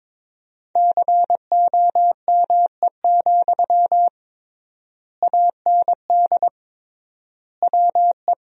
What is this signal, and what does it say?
Morse code, heard as a keyed tone, about 22 wpm, 700 Hz. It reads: COME, AND WE
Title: 3x Word-Spacing - Sentences from Top 500 Words - 22wpm